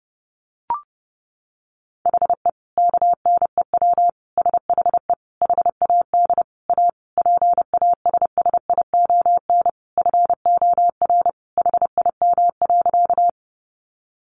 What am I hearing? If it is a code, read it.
Morse code, heard as a keyed tone, about 30 wpm, 700 Hz. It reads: HE KNEW SHE HAD A PASSION FOR HIM.
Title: Sentences from Top 1300 Words 30wpm